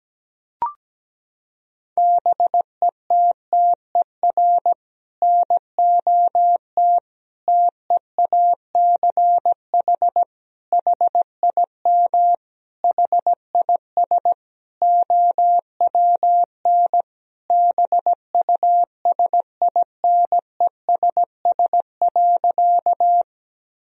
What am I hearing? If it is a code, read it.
Morse code, heard as a keyed tone, about 17 wpm, 700 Hz. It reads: BETTER NOT TEACH HIM HIS OWN BUSINESS.
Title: Sentences from Top 1000 Words - Review Part 2 - 17wpm